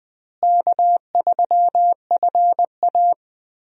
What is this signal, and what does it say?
Morse code, heard as a keyed tone, about 20 wpm, 700 Hz. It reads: K3FA
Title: USA Call Signs - 4 Characters - Repeated 3x - 20wpm